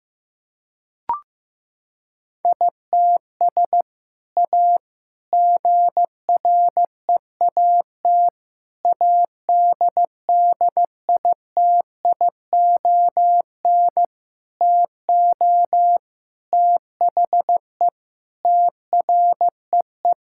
Morse code, heard as a keyed tone, about 15 wpm, 700 Hz. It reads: ITS A GREAT ADDITION TO THE TREE
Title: Sentences from Top 1000 Words - Review Part 6 - 15wpm